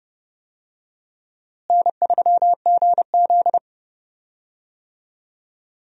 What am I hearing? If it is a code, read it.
Morse code, heard as a keyed tone, about 32 wpm, 700 Hz. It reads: N3GZ
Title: USA Call Signs - 4 Characters - Repeated 2x - 30wpm